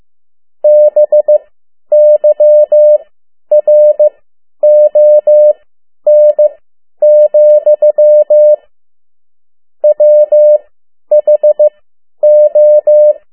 Morse code, heard as a keyed tone, about 15 wpm, spaced as for 10 wpm, 600 Hz. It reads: BYRON, WHO
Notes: Morse code 10 WPM